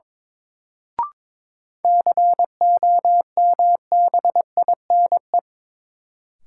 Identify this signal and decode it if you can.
Morse code, heard as a keyed tone, about 21 wpm, 700 Hz. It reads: COMBINE
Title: Sentences from Top 1600 Words - New Word Drill Alt - 22wpm